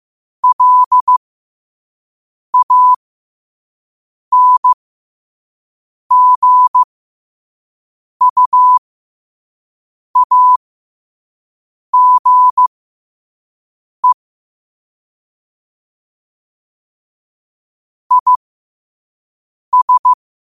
Morse code, heard as a keyed tone, about 15 wpm, spaced as for 5 wpm, 1000 Hz. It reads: LANGUAGE IS
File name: qotdfiles / 5wpm